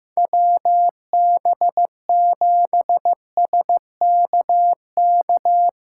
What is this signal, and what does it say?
Morse code, heard as a keyed tone, about 15 wpm, 700 Hz. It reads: WB7SKK
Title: USA Callsign DE Callsign - 6 Characters - 15wpm